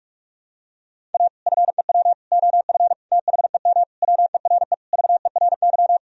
Morse code, heard as a keyed tone, about 46 wpm, 700 Hz. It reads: A FEW OF THEM WERE VERY
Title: Sentences from Top 1000 Words - Review Part 1 - 45wpm